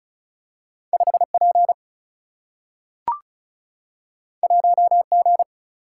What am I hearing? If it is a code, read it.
Morse code, heard as a keyed tone, about 36 wpm, 700 Hz. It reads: 5 P 1 G
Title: Single-Letter Single-Number 35wpm